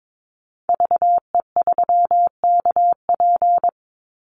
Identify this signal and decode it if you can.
Morse code, heard as a keyed tone, about 22 wpm, 690 Hz. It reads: VE3KP
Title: Call Signs Repeated 3x - 22wpm